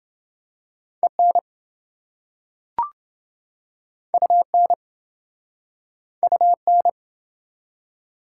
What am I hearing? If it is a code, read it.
Morse code, heard as a keyed tone, about 26 wpm, 700 Hz. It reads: EN UN UN